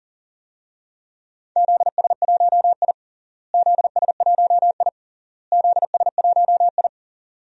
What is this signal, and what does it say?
Morse code, heard as a keyed tone, about 40 wpm, 700 Hz. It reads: ZS1I ZS1I ZS1I
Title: DXCC-100 Call Signs - 4 Characters - Repeated 3x - 40wpm